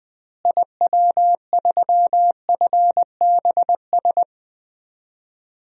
Morse code, heard as a keyed tone, about 20 wpm, 700 Hz. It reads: IW3FBS
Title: DXCC-100 Call Signs - 6 Characters - Repeated 2x - 20wpm